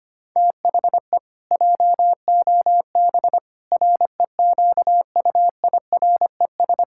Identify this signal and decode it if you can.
Morse code, heard as a keyed tone, about 25 wpm, 700 Hz. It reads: THE JOB REQUIRES
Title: Pangram Challenge v2 - 25wpm